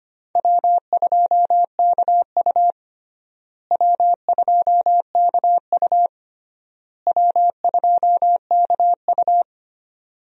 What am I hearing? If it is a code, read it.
Morse code, heard as a keyed tone, about 25 wpm, 700 Hz. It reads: W2KU W2KU W2KU
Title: USA Call Signs - 4 Characters - Repeated 3x - 25wpm